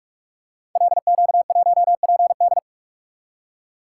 Morse code, heard as a keyed tone, about 44 wpm, 700 Hz. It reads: LQ1PD
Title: Code Groups - 5 Letter:Numbers - 45wpm